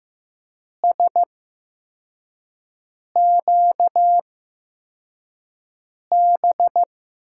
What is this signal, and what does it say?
Morse code, heard as a keyed tone, about 15 wpm, spaced as for 4 wpm, 700 Hz. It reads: SQB
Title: USA Call Signs - 6 Characters - 8x Character Spacing - 15wpm